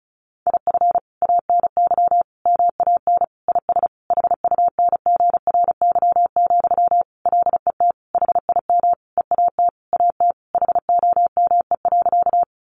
Morse code, heard as a keyed tone, about 34 wpm, 700 Hz. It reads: IF ANY MAN IS HUNGRY, LET HIM EAT AT HOME.
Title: Sentences from Top 1400 Words 35wpm